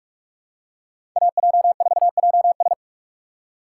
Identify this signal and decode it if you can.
Morse code, heard as a keyed tone, about 46 wpm, 700 Hz. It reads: AJ4JS
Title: USA Call Signs - 5 Characters - 6x Character Spacing - 45wpm